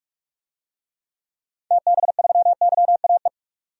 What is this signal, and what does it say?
Morse code, heard as a keyed tone, about 46 wpm, 700 Hz. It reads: TB3YAE
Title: DXCC-100 Call Signs - 6 Characters - 45wpm